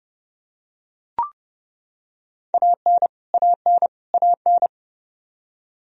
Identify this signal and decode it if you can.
Morse code, heard as a keyed tone, about 32 wpm, 700 Hz. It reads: AN AN AN
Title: Top 100 Words Repeated 30wpm